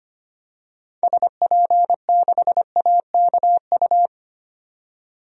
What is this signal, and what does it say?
Morse code, heard as a keyed tone, about 25 wpm, 700 Hz. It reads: SP6AKU
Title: DXCC-100 Call Signs - 6 Characters - 25wpm